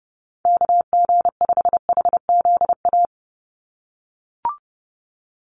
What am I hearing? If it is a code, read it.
Morse code, heard as a keyed tone, about 30 wpm, 700 Hz. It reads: KG5HZA
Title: USA Call Signs 30wpm